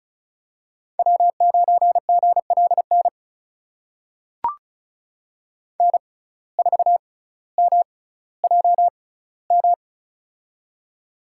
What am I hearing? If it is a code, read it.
Morse code, heard as a keyed tone, about 34 wpm, spaced as for 18 wpm, 700 Hz. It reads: W9GLN N4MJM
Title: USA Call Signs - 5 Characters - 6x Character Spacing - 35wpm